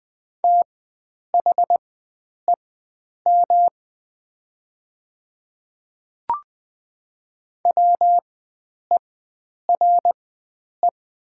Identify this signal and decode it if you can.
Morse code, heard as a keyed tone, about 20 wpm, spaced as for 9 wpm, 700 Hz. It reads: THEM WERE
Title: 4x Character Spacing - Top 100 Words - 20wpm